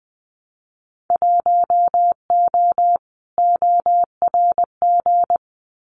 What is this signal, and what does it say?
Morse code, heard as a keyed tone, about 20 wpm, 700 Hz. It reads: 1O ORG
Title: Winter Field Day Exchanges 20wpm